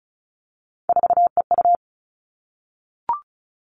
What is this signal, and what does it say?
Morse code, heard as a keyed tone, about 34 wpm, 700 Hz. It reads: 4EU
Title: All Cut Numbers - 3-Digit 35wpm